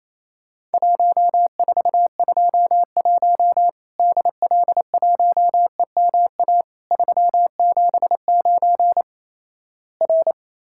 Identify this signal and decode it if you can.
Morse code, heard as a keyed tone, about 28 wpm, 700 Hz. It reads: 1421 DL1EMA 379 R
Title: Worked All Europe DX Contest - QTC Exchange - 28wpm